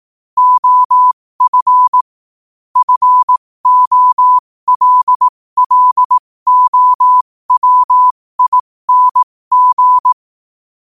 Morse code, heard as a keyed tone, about 18 wpm, spaced as for 15 wpm, 1000 Hz. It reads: OF FOLLOWING
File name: qotdfiles / 15wpm